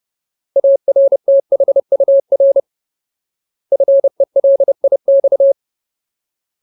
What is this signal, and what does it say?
Morse code, heard as a keyed tone, about 30 wpm, 550 Hz. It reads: ARTHUR FELIX
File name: Ultimate Word List / 30WPM Ultimate